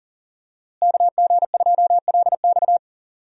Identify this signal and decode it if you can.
Morse code, heard as a keyed tone, about 40 wpm, 700 Hz. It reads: KG2LX
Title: USA Call Signs - 5 Characters - Repeated 3x - 40wpm